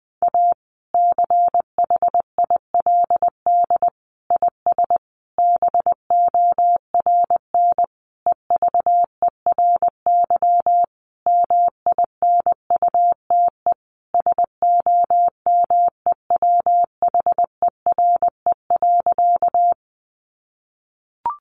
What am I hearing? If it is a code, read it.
Morse code, heard as a keyed tone, about 20 wpm, 700 Hz. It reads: A CHILD IS BORN EVERY MINUTE SOMEWHERE.
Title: Sentences from Top 1400 Words 20wpm